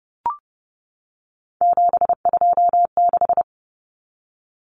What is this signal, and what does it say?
Morse code, heard as a keyed tone, about 30 wpm, 700 Hz. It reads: 726
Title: Common Cut Numbers - 3-Digit 30wpm